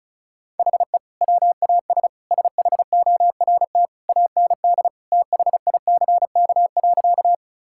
Morse code, heard as a keyed tone, about 34 wpm, 700 Hz. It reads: HE WAS SHORT AND THICK.
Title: Sentences from Top 1600 Words 35wpm